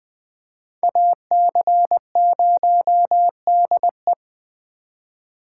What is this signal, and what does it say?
Morse code, heard as a keyed tone, about 25 wpm, 700 Hz. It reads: AC0DE
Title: Speed Racing - USA Call Signs 20wpm